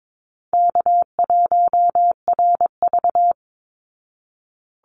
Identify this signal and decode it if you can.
Morse code, heard as a keyed tone, about 22 wpm, 700 Hz. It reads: K1RV
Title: Call Signs Repeated 3x - 22wpm